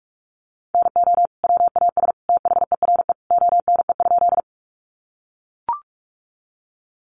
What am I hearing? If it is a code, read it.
Morse code, heard as a keyed tone, about 44 wpm, 700 Hz. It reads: NO WAS THERE ONE?
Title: Sentences from Top 100 Words 45wpm